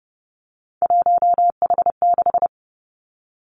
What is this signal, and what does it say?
Morse code, heard as a keyed tone, about 30 wpm, 700 Hz. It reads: 1H6
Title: Three Challenging Letter-Number Combinations 30wpm